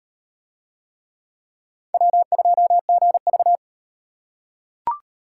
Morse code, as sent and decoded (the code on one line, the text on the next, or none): .-- ..--- --. ...-
W2GV